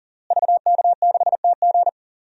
Morse code, heard as a keyed tone, about 40 wpm, 700 Hz. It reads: VK6TG
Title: DXCC-100 Call Signs - 5 Characters - Repeated 3x - 40wpm